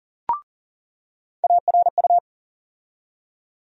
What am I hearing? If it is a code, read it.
Morse code, heard as a keyed tone, about 40 wpm, 700 Hz. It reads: ARU
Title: USA Call Sign Postfixes - 3 Letters 40wpm